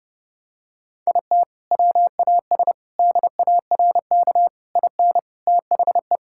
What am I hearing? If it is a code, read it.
Morse code, heard as a keyed tone, about 32 wpm, 700 Hz. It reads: IT WAS DARK IN THE
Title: Sentences from Top 1600 Words 30wpm